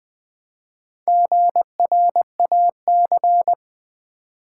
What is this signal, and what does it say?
Morse code, heard as a keyed tone, about 20 wpm, 700 Hz. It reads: GRAC
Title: Build A Word - 1 Letter At A Time - 5-Letter Words - 20wpm